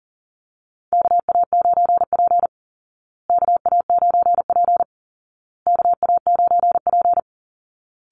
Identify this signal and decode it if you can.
Morse code, heard as a keyed tone, about 40 wpm, 700 Hz. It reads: KA9P KA9P KA9P
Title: Call Signs Repeated 2x - 40wpm